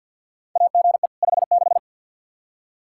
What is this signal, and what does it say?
Morse code, heard as a keyed tone, about 48 wpm, 700 Hz. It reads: AGE 56
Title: K1USN SST QSO Practice 50wpm